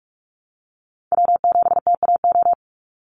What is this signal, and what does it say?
Morse code, heard as a keyed tone, about 46 wpm, 700 Hz. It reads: W7TAO
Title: NA SOTA Chaser Callsigns 45wpm